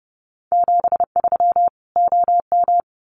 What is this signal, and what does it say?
Morse code, heard as a keyed tone, about 30 wpm, 700 Hz. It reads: 73 OM
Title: Common QSO Phrases 30wpm